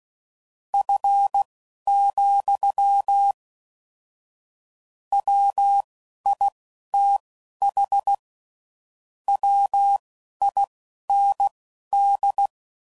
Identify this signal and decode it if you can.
Morse code, heard as a keyed tone, about 16 wpm, spaced as for 11 wpm, 800 Hz. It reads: F, WITH WIND
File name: CW Code Groups @10WPM / Weather Sets / Weather 5 12